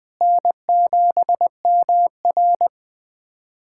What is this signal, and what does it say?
Morse code, heard as a keyed tone, about 20 wpm, 700 Hz. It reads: N7MR
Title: USA Callsign DE Callsign - 4 Characters - 20wpm